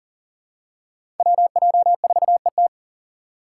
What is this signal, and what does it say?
Morse code, heard as a keyed tone, about 38 wpm, 700 Hz. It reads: WJ4ET
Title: USA Call Signs - 5 Characters - 4x Character Spacing - 40wpm